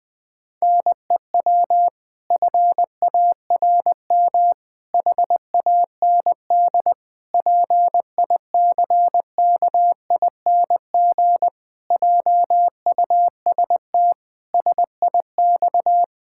Morse code, heard as a keyed tone, about 20 wpm, 700 Hz. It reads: NEW FARM HAND PICKING JUST SIX
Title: Pangram Challenge v2 - 20wpm